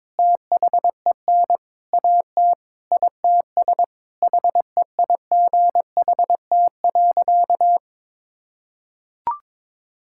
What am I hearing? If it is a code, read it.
Morse code, heard as a keyed tone, about 22 wpm, 700 Hz. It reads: THEN AT ITS HEIGHT.
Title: Sentences from Top 1600 Words 22wpm